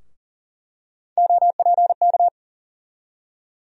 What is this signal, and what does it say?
Morse code, heard as a keyed tone, about 40 wpm, 700 Hz. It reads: OPK
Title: USA Call Sign Postfixes - 3 Letters 40wpm